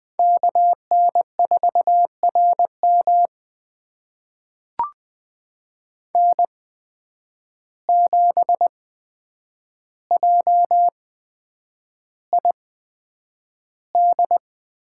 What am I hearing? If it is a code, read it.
Morse code, heard as a keyed tone, about 20 wpm, spaced as for 9 wpm, 700 Hz. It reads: KN4RM N7JID
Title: USA Call Signs - 5 Characters - 8x Character Spacing - 20wpm